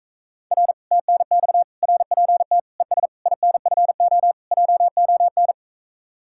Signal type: Morse code

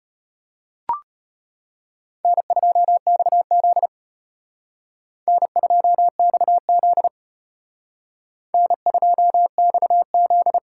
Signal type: Morse code

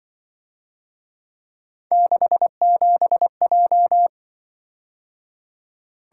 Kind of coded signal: Morse code

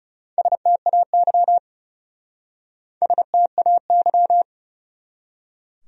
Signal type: Morse code

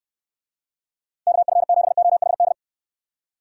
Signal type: Morse code

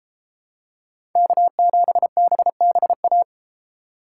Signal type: Morse code

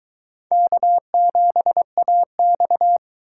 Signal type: Morse code

